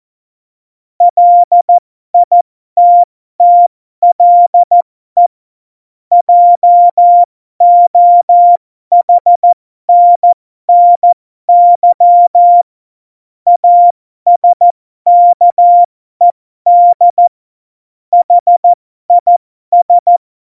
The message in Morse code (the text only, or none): LITTLE JOHNNY ASKED HIS